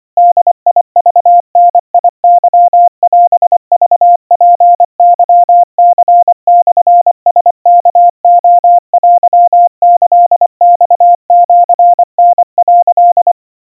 Morse code (Japanese Wabun code), heard as a keyed tone, about 24 wpm, 700 Hz. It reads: ボクダケオクツケニモラワレテキマシタ」
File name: 90 cpm 700 Hz Crispy